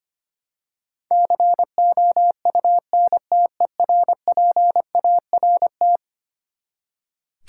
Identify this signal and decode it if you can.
Morse code, heard as a keyed tone, about 25 wpm, 700 Hz. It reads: COUNTERPART
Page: morse-code-ninja-practice-25wpm